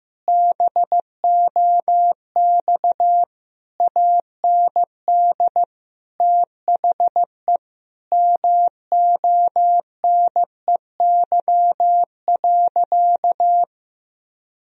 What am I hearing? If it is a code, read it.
Morse code, heard as a keyed tone, about 15 wpm, 700 Hz. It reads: BOX AND THE MONEY.
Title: Sentences from Top 1000 Words - Review Part 2 - 15wpm